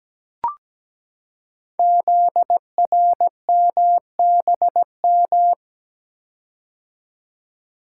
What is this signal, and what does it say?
Morse code, heard as a keyed tone, about 17 wpm, 700 Hz. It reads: ZRMBM
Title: Code Groups - 5 Letters - 17wpm